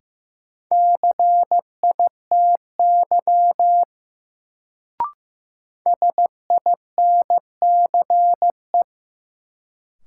Morse code, heard as a keyed tone, about 15 wpm, 700 Hz. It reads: CITY SINCE
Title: Sentences from Top 300 Words - New Word Drill - 15wpm